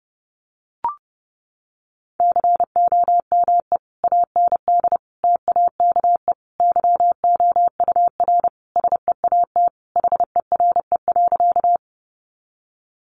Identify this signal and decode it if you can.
Morse code, heard as a keyed tone, about 30 wpm, 700 Hz. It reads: COME AND TAKE YOUR SEAT HERE.
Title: Sentences from Top 1500 Words 30wpm